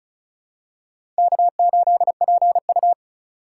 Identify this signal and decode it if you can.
Morse code, heard as a keyed tone, about 36 wpm, 700 Hz. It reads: K8PU
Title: USA Call Signs - 4x Character Spacing - 35wpm